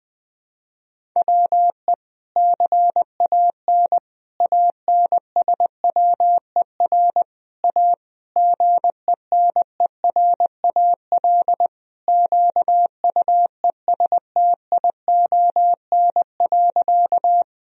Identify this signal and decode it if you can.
Morse code, heard as a keyed tone, about 20 wpm, 700 Hz. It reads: WE CAN ANSWER A GENERAL QUESTION.
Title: Sentences from Top 1000 Words - Review Part 3 - 20wpm